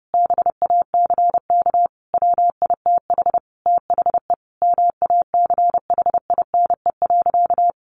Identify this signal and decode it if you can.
Morse code, heard as a keyed tone, about 30 wpm, 700 Hz. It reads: BACK WITH THE MACHINE.
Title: A Sound Of Thunder 30wpm